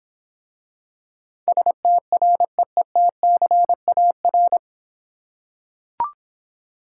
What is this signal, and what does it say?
Morse code, heard as a keyed tone, about 26 wpm, 700 Hz. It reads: STREETCAR